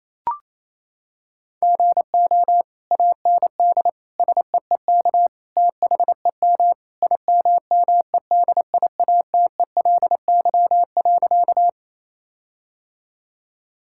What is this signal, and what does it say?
Morse code, heard as a keyed tone, about 28 wpm, 700 Hz. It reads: GO AND SEEK THEM IMMEDIATELY.
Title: Sentences from Top 1600 Words 28wpm